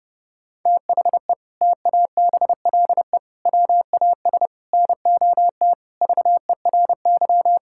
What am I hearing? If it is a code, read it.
Morse code, heard as a keyed tone, about 32 wpm, 700 Hz. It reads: THE TABLE WAS NOT VERY
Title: Sentences from Top 1000 Words - Review Part 6 - 30wpm